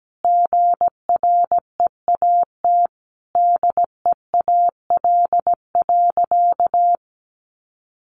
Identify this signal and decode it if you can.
Morse code, heard as a keyed tone, about 17 wpm, 700 Hz. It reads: GREAT DEAL.
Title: Sentences from Top 1300 Words 17wpm